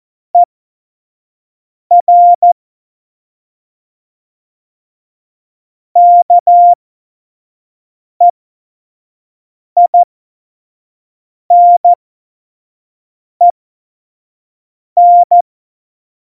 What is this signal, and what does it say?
Morse code, heard as a keyed tone, about 14 wpm, spaced as for 5 wpm, 700 Hz. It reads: ER KEINEN
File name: Morsekurs / Witze / Witz 07. November 2017, 25 F70